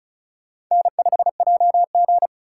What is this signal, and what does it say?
Morse code, heard as a keyed tone, about 43 wpm, 700 Hz. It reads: N5JG
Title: Speed Racing - USA 1x2 Call Signs 35wpm